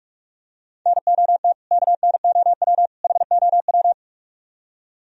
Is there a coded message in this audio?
Yes: Morse code from 0.9 to 3.9 s, heard as a keyed tone, about 45 wpm, 700 Hz. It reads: NOT KNOW HOW